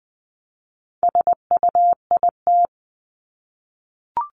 Morse code, heard as a keyed tone, about 20 wpm, 700 Hz. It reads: SUIT
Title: Top 1501-1750 Words 20wpm